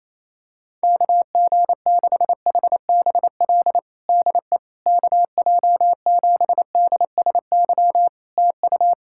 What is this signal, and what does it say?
Morse code, heard as a keyed tone, about 27 wpm, 700 Hz. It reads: KG6HBL DE KJ7DSY TU
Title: K1USN SST QSO Practice 28wpm